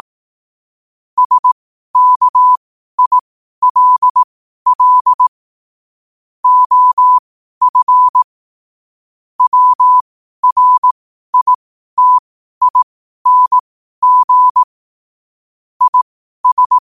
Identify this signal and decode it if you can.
Morse code, heard as a keyed tone, about 18 wpm, spaced as for 12 wpm, 1000 Hz. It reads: SKILL OF WRITING IS
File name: qotdfiles / 12wpm